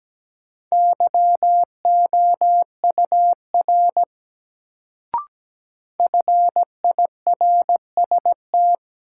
Morse code, heard as a keyed tone, about 17 wpm, 700 Hz. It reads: YOUR FIRST
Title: Sentences from Top 100 Words - New Word Drill - 17wpm